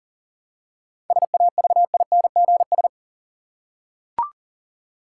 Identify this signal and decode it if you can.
Morse code, heard as a keyed tone, about 37 wpm, 700 Hz. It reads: SAVINGS
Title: Seven Letter Words - 40wpm